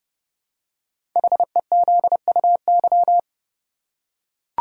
Morse code, heard as a keyed tone, about 30 wpm, 700 Hz. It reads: HEZUY
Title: Code Groups - 5 Letters - 30wpm